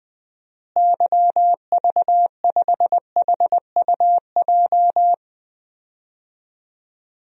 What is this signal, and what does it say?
Morse code, heard as a keyed tone, about 21 wpm, 700 Hz. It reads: YV5HUJ
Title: DXCC-100 Call Signs - 6 Characters - Repeated 2x - 20wpm